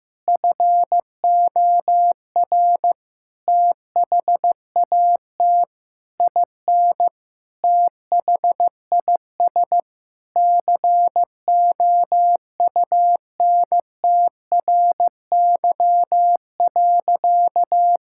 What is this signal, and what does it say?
Morse code, heard as a keyed tone, about 15 wpm, 700 Hz. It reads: FOR THAT IN THIS COUNTRY.
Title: Sentences from Top 1000 Words - Review Part 5 - 15wpm